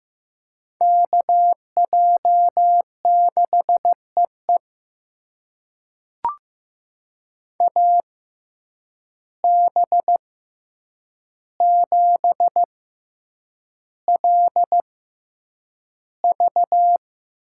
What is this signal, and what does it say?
Morse code, heard as a keyed tone, about 15 wpm, spaced as for 8 wpm, 700 Hz. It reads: KJ6EE AB7LV
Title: USA Call Signs - 5 Characters - 6x Character Spacing - 15wpm